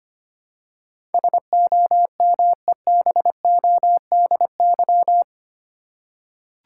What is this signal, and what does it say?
Morse code, heard as a keyed tone, about 25 wpm, 700 Hz. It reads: SOMEBODY
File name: 8 Letter Words - First 3 Letters Given - 25wpm